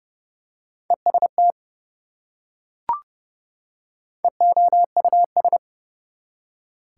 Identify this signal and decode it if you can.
Morse code, heard as a keyed tone, about 30 wpm, 700 Hz. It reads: EST EOUS
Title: Prefixes and Suffixes 30wpm